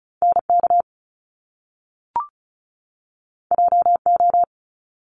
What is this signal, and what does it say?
Morse code, heard as a keyed tone, about 36 wpm, 700 Hz. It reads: NK JO
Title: Top Two Letter Combinations in Words 35wpm